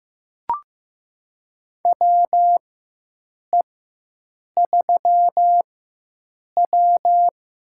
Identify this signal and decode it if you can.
Morse code, heard as a keyed tone, about 15 wpm, spaced as for 7 wpm, 700 Hz. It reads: WE3W
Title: USA Call Signs - 4 Characters - 4x Character Spacing - 15wpm